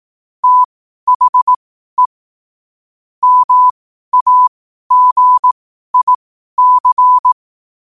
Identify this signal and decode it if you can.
Morse code, heard as a keyed tone, about 18 wpm, spaced as for 12 wpm, 1000 Hz. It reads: THE MAGIC
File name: qotdfiles / 12wpm